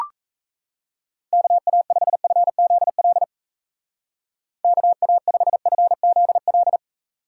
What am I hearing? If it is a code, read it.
Morse code, heard as a keyed tone, about 36 wpm, 700 Hz. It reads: KA5FZL KA5FZL